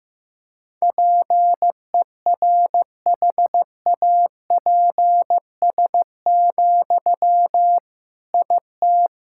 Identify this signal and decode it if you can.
Morse code, heard as a keyed tone, about 15 wpm, 700 Hz. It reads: PERHAPS, IT
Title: Sentences from Top 1000 Words - Review Part 6 - 15wpm